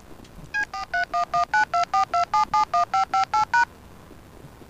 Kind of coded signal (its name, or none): DTMF